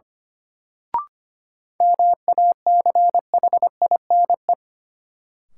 Morse code, heard as a keyed tone, about 25 wpm, 700 Hz. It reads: MACHINE
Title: Sentences from Top 1000 Words - Review Part 2 - New Word Drill - 25wpm